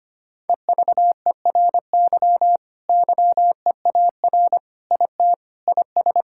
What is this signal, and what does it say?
Morse code, heard as a keyed tone, about 25 wpm, 700 Hz. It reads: EVERY YEAR IT IS
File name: practice / 25wpm / Sentences with 9-Word Phrases - 25wpm